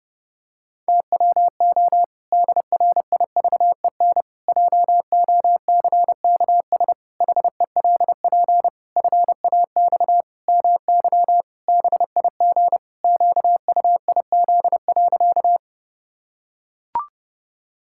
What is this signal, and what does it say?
Morse code, heard as a keyed tone, about 30 wpm, 700 Hz. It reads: TWO DRIVEN JOCKS HELP FAX MY BIG QUIZ.
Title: Pangram Challenge v2 - 30wpm